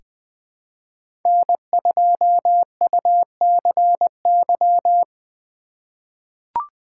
Morse code, heard as a keyed tone, about 21 wpm, 700 Hz. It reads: N2UCY
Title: USA Call Signs - 5 Characters - 4x Character Spacing - 20wpm